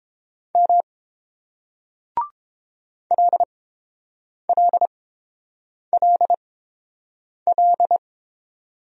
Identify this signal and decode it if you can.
Morse code, heard as a keyed tone, about 31 wpm, spaced as for 7 wpm, 700 Hz. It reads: M LLLL